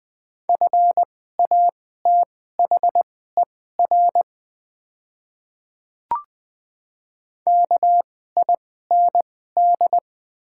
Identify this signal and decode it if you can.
Morse code, heard as a keyed tone, about 20 wpm, spaced as for 14 wpm, 690 Hz. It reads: FATHER KIND
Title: 2x Character Spacing - Top 500 Words - 20wpm